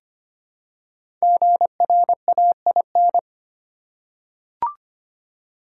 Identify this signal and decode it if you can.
Morse code, heard as a keyed tone, about 25 wpm, 700 Hz. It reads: GRAIN
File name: 5 Letter Words - First 3 Letters Given - 25wpm